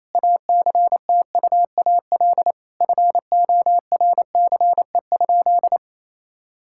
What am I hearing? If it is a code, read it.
Morse code, heard as a keyed tone, about 28 wpm, 700 Hz. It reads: ACTUAL FORCE?
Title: Sentences from Top 1000 Words - Review Part 5 - 28wpm